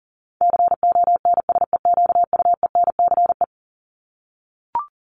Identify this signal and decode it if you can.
Morse code, heard as a keyed tone, about 40 wpm, 700 Hz. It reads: CONSEQUENCE
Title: LY Suffix - Top 100 Words with Base Words 40wpm